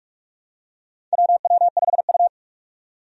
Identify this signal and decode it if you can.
Morse code, heard as a keyed tone, about 46 wpm, 700 Hz. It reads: WW5U
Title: USA Call Signs - 4 Characters - 45wpm